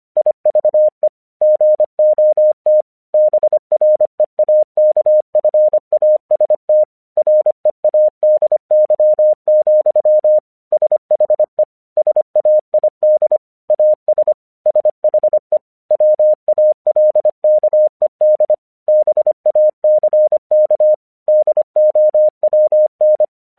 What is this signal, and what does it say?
Morse code, heard as a keyed tone, about 25 wpm, 610 Hz. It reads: IVE GOT BREAKFAST READY, SHE SAID AS SHE WALKED BACK DOWN
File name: 1 English 25wpm / AQC36